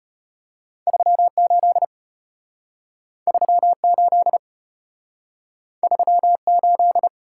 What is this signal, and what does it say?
Morse code, heard as a keyed tone, about 33 wpm, 700 Hz. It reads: 38 38 38